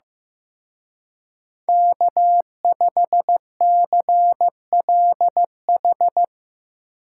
Morse code, heard as a keyed tone, about 16 wpm, 700 Hz. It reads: K5CLH
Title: Speed Racing - USA 1x3 Call Signs 15wpm